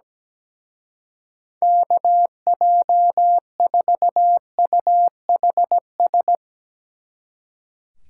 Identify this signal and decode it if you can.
Morse code, heard as a keyed tone, about 19 wpm, 700 Hz. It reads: KJ4UHS
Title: Speed Racing - USA 2x3 Call Signs 17wpm